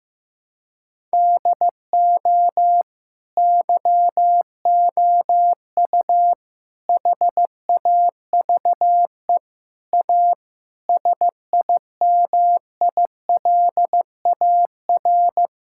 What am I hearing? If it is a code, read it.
Morse code, heard as a keyed tone, about 16 wpm, 700 Hz. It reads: DO YOU HAVE A SIMILAR
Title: Sentences from Top 1000 Words - Review Part 5 - 15wpm